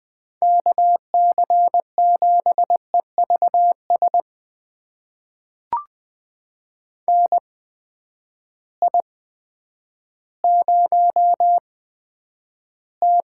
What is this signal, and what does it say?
Morse code, heard as a keyed tone, about 20 wpm, spaced as for 10 wpm, 700 Hz. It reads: KC7EVS NI0T
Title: USA Call Signs - 8x Character Spacing - 20wpm